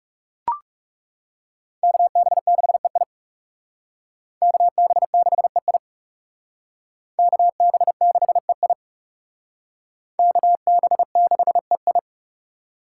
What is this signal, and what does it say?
Morse code, heard as a keyed tone, about 37 wpm, 700 Hz. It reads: KB6EI KB6EI KB6EI KB6EI
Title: Speed Racing - USA 2x2 Call Signs 30wpm